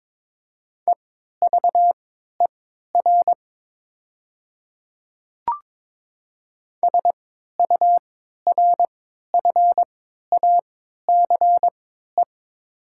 Morse code, heard as a keyed tone, about 22 wpm, spaced as for 12 wpm, 700 Hz. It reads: EVER SURFACE
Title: 3x Character Spacing - Top 500 Words - 22wpm